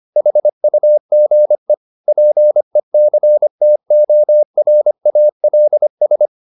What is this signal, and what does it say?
Morse code, heard as a keyed tone, about 25 wpm, 600 Hz. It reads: HUGE PECTORALS